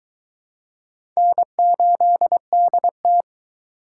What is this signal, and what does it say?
Morse code, heard as a keyed tone, about 21 wpm, 700 Hz. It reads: N8DT